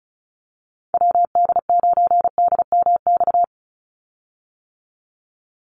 Morse code, heard as a keyed tone, about 34 wpm, 700 Hz. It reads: WD9DMX
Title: USA Call Signs 35wpm